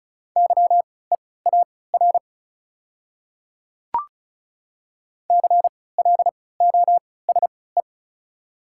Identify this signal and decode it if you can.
Morse code, heard as a keyed tone, about 36 wpm, spaced as for 20 wpm, 690 Hz. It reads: YEAR CLOSE
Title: 3x Character Spacing - Top 500 Words - 35wpm